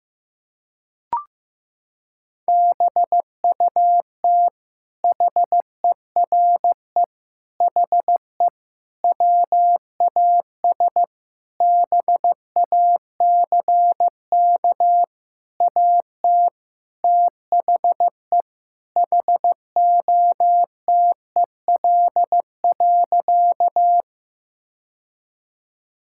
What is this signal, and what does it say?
Morse code, heard as a keyed tone, about 14 wpm, 700 Hz. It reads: BUT HERE HE WAS BACK AT THE HOTEL.
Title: Sentences from Top 1600 Words 15wpm